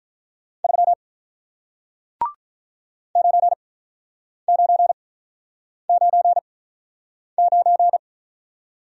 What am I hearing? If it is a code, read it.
Morse code, heard as a keyed tone, about 48 wpm, spaced as for 9 wpm, 700 Hz. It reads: 3 9999